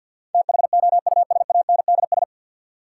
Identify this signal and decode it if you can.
Morse code, heard as a keyed tone, about 50 wpm, 700 Hz. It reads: THOUSANDS
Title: Sets of 4 Words - New Word Drill - 50wpm